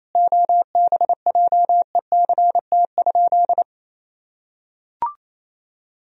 Morse code, heard as a keyed tone, about 28 wpm, 700 Hz. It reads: OBJECT?
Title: Sentences from Top 1000 Words - Review Part 4 - 28wpm